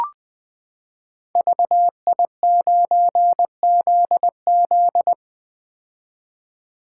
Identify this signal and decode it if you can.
Morse code, heard as a keyed tone, about 20 wpm, 700 Hz. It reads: VI9ZZ
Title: Code Groups - 5 Letter:Numbers - 20wpm